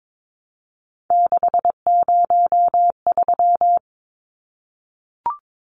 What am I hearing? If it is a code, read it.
Morse code, heard as a keyed tone, about 22 wpm, 700 Hz. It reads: 603
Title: Numbers - 3-Digit 22wpm